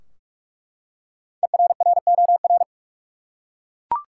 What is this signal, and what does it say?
Morse code, heard as a keyed tone, about 42 wpm, 700 Hz. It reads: ERROR
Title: Five Letter Words - 45wpm